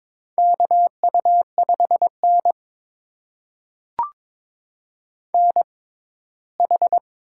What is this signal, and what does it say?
Morse code, heard as a keyed tone, about 22 wpm, spaced as for 15 wpm, 700 Hz. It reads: KU5N NH
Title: USA Call Signs - 4 Characters - 6x Character Spacing - 22wpm